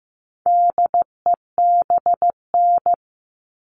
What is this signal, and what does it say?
Morse code, heard as a keyed tone, about 15 wpm, 700 Hz. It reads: DEBN
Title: All Cut Numbers - 4-Digit 15wpm